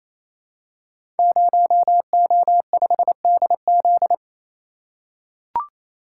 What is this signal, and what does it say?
Morse code, heard as a keyed tone, about 29 wpm, 700 Hz. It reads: 0O5DZ
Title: Code Groups - 5 Letter:Numbers - 28wpm